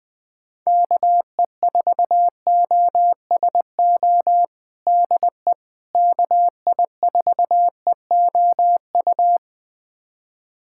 Morse code, heard as a keyed tone, about 20 wpm, 700 Hz. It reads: KE4OSO DE KI4EOU
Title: USA Callsign DE Callsign - 6 Characters - 20wpm